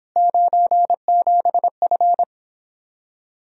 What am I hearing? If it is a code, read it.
Morse code, heard as a keyed tone, about 31 wpm, 700 Hz. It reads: 97F